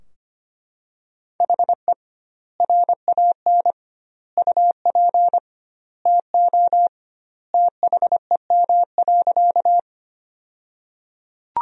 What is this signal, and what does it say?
Morse code, heard as a keyed tone, about 25 wpm, 700 Hz. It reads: HE RAN UP TO THEM.
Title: 2x Word-Spacing - Sentences from Top 500 Words - 25wpm